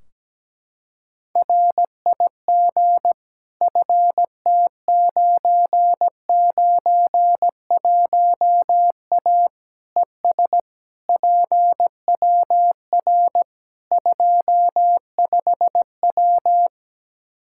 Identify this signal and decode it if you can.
Morse code, heard as a keyed tone, about 17 wpm, 700 Hz. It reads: RIG FT991A ES PWR 25W
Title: Practice QSO - Exchange 2 - Info - 17wpm